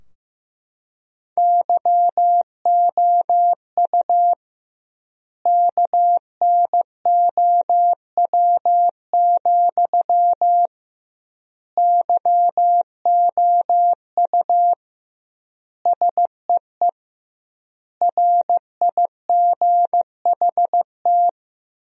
Morse code, heard as a keyed tone, about 15 wpm, 700 Hz. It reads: YOU KNOW, YOU SEE RIGHT
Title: 2x Word-Spacing - Sentences from Top 200 Words - 15wpm